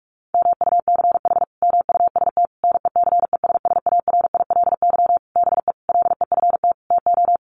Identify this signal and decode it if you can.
Morse code, heard as a keyed tone, about 44 wpm, 700 Hz. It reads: MUCH MUST NECESSARILY BE LEFT TO
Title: Sentences from Top 1400 Words 45wpm